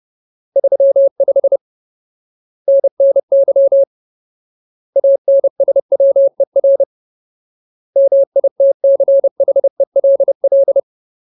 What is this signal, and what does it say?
Morse code, heard as a keyed tone, about 30 wpm, 550 Hz. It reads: 35 NNY ANSWER MITCHELL
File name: Ultimate Word List / 30WPM Ultimate